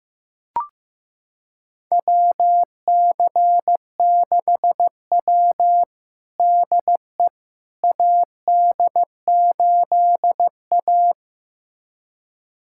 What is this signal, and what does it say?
Morse code, heard as a keyed tone, about 15 wpm, 700 Hz. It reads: WC6W DE AD8A
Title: USA Callsign DE Callsign - 4 Characters - 15wpm